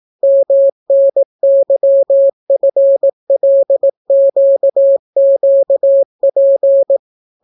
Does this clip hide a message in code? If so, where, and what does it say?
Yes: Morse code from 0.2 to 7.0 s, heard as a keyed tone, about 18 wpm, 550 Hz. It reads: MNYFLQQP